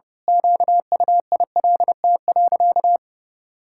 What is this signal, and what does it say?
Morse code, heard as a keyed tone, about 30 wpm, 700 Hz. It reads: QUILT.
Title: Pangram Challenge v2 - 30wpm